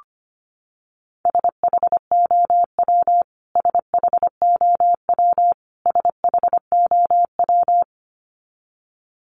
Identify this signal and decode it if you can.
Morse code, heard as a keyed tone, about 25 wpm, 700 Hz. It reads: SHOW SHOW SHOW
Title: Top 200 Words Repeated 25wpm